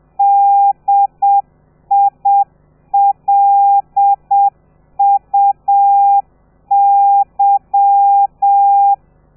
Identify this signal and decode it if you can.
Morse code, heard as a keyed tone, about 7 wpm, 800 Hz. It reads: DILUY
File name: telegrafia / i2tfj / Lez13 30